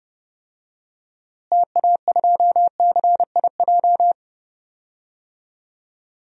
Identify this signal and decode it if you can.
Morse code, heard as a keyed tone, about 30 wpm, 700 Hz. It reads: TA2CIJ
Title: DXCC-100 Call Signs - 6 Characters - Repeated 3x - 30wpm